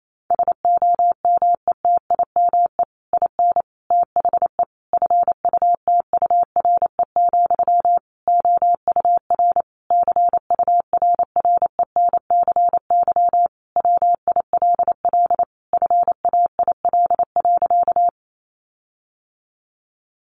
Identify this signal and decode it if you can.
Morse code, heard as a keyed tone, about 28 wpm, 700 Hz. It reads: SOMETIME IN THE FUTURE, OUR CURRENCY WILL FAIL.
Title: Sentences from Top 1500 Words 28wpm